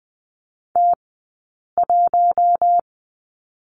Morse code, heard as a keyed tone, about 20 wpm, 700 Hz. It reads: T 1
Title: 171 Lesson 28d - Number 9 - Two Characters 20wpm